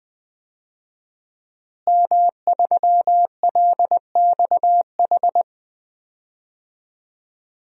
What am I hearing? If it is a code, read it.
Morse code, heard as a keyed tone, about 20 wpm, 700 Hz. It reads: M3LXH
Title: DXCC-100 Call Signs - 5 Characters - Repeated 2x - 20wpm